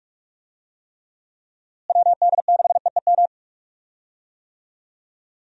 Morse code, heard as a keyed tone, about 46 wpm, 700 Hz. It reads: WD6EEM
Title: USA Call Signs - 6 Characters - Repeated 2x - 45wpm